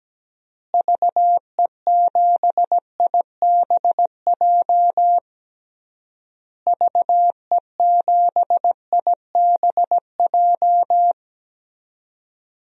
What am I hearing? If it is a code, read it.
Morse code, heard as a keyed tone, about 17 wpm, 700 Hz. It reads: VE7IBJ VE7IBJ
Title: DXCC-100 Call Signs - 6 Characters - Repeated 3x - 17wpm